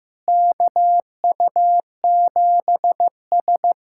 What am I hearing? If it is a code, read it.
Morse code, heard as a keyed tone, about 15 wpm, 700 Hz. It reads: KU7S
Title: USA Callsign DE Callsign - 4 Characters - 15wpm